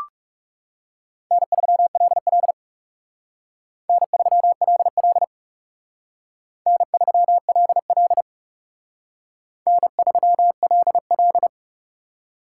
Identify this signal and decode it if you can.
Morse code, heard as a keyed tone, about 37 wpm, 700 Hz. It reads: N3LL N3LL N3LL N3LL
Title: Speed Racing - USA 1x2 Call Signs 30wpm